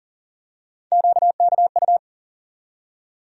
Morse code, heard as a keyed tone, about 40 wpm, 700 Hz. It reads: QKU
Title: USA Call Sign Postfixes - 3 Letters 40wpm